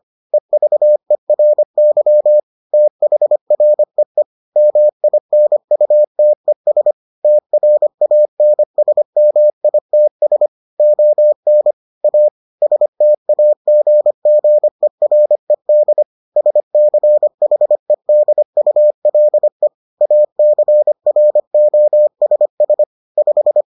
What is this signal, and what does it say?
Morse code, heard as a keyed tone, about 25 wpm, 600 Hz. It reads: EVERY THREE MINUTES TRANSMITS ON A STAGGERED SCHEDULE ACROSS 5